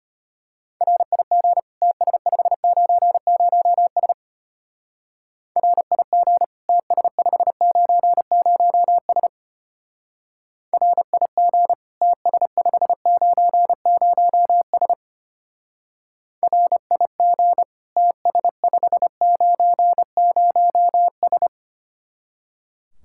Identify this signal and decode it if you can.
Morse code, heard as a keyed tone, about 31 wpm, 700 Hz. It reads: RIG TS590S RIG TS590S RIG TS590S RIG TS590S
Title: Speed-Racing Common QSO Phrases - 25wpm